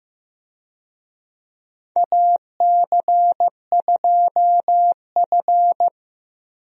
Morse code, heard as a keyed tone, about 15 wpm, 700 Hz. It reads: AC2F
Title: USA Call Signs - 4 Characters - Repeated 3x - 15wpm